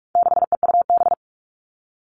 Morse code, heard as a keyed tone, about 44 wpm, 700 Hz. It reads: 6EUB
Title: All Cut Numbers - 4-Digit 45wpm